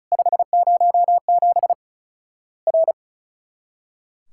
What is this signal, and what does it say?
Morse code, heard as a keyed tone, about 36 wpm, 700 Hz. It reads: 507 R
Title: Worked All Europe DX Contest - QTC Exchange - 35wpm